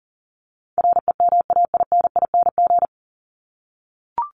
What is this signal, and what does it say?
Morse code, heard as a keyed tone, about 40 wpm, 700 Hz. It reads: REMAINING
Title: ING Suffix - Top 100 Words with Base Words 40wpm